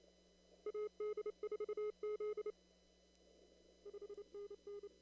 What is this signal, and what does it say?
Morse code, heard as a keyed tone, about 32 wpm, 410 Hz. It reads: AD4Z 5NN
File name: CW / kt5j wpx cw-op-k5pi